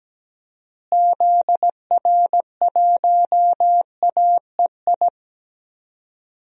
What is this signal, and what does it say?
Morse code, heard as a keyed tone, about 17 wpm, 700 Hz. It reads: ZR1AEI
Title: DXCC-100 Call Signs - 6 Characters - Repeated 3x - 17wpm